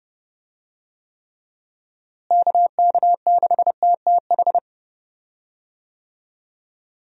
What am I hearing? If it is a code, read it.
Morse code, heard as a keyed tone, about 31 wpm, 700 Hz. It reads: KK6TTH
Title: USA Call Signs - 6 Characters - Repeated 2x - 30wpm